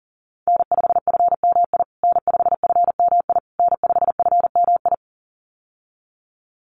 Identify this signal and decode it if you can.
Morse code, heard as a keyed tone, about 40 wpm, 700 Hz. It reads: N5FMI N5FMI N5FMI
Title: USA Call Signs Repeated 3x 40wpm